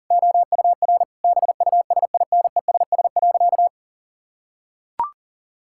Morse code, heard as a keyed tone, about 40 wpm, 700 Hz. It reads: OUR BUSINESS.
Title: Sentences from Top 1000 Words - Review Part 6 - 40wpm